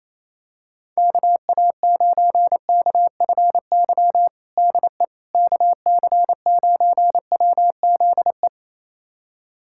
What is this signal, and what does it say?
Morse code, heard as a keyed tone, about 28 wpm, 700 Hz. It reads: KA9KFY DE KC9WZE
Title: USA Callsign DE Callsign - 6 Characters - 28wpm